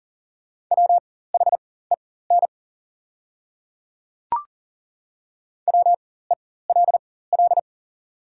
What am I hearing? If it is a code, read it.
Morse code, heard as a keyed tone, about 39 wpm, spaced as for 19 wpm, 690 Hz. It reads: WHEN WELL
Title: 4x Character Spacing - Top 200 Words - 40wpm